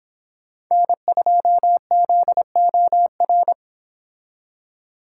Morse code, heard as a keyed tone, about 26 wpm, 700 Hz. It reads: N2ZOR